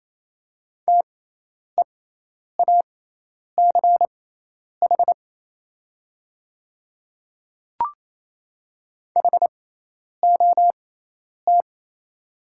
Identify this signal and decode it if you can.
Morse code, heard as a keyed tone, about 27 wpm, spaced as for 10 wpm, 690 Hz. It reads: TEACH HOT
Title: 6x Character Spacing - Top 500 Words - 28wpm